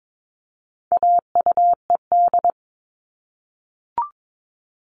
Morse code, heard as a keyed tone, about 22 wpm, 700 Hz. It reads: AUED
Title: All Cut Numbers - 4-Digit 22wpm